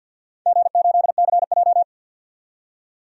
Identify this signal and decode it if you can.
Morse code, heard as a keyed tone, about 52 wpm, 700 Hz. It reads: G8CJ
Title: DXCC-100 Call Signs - 4 Characters - 50wpm